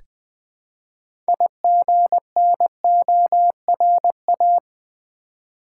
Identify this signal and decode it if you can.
Morse code, heard as a keyed tone, about 20 wpm, 700 Hz. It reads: IGNORA
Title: Build A Word - 3 Letters At A Time - 9-Letter Words - 20wpm